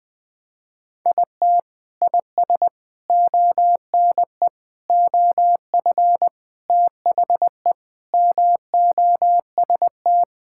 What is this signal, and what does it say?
Morse code, heard as a keyed tone, about 20 wpm, 700 Hz. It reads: IT IS ONE OF THE MOST
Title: Sentences with 3-Word Phrases - 20wpm